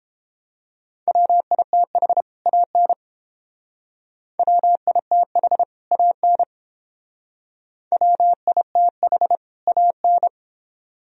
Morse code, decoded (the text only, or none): WITH AN WITH AN WITH AN